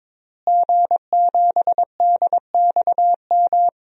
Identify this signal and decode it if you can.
Morse code, heard as a keyed tone, about 22 wpm, 700 Hz. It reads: G7DXM
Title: DXCC-100 Call Signs - 5 Characters - Repeated 2x - 22wpm